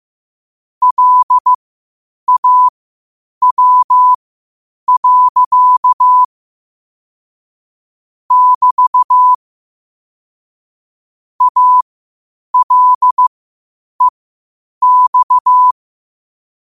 Morse code, heard as a keyed tone, about 15 wpm, spaced as for 8 wpm, 1000 Hz. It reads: LAW. <BT> ALEX